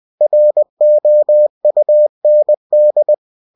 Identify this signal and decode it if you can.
Morse code, heard as a keyed tone, about 20 wpm, 600 Hz. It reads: ROUND
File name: cw-books / winnie-the-pooh / winnie-20wpm-page-0096